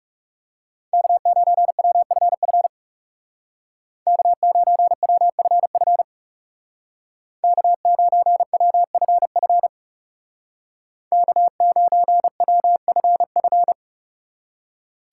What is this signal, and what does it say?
Morse code, heard as a keyed tone, about 39 wpm, 700 Hz. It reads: K9WFF K9WFF K9WFF K9WFF